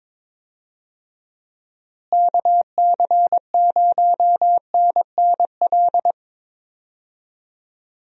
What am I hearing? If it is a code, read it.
Morse code, heard as a keyed tone, about 23 wpm, 700 Hz. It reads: KC0NNL
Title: USA Call Signs - 6 Characters - Repeated 3x - 22wpm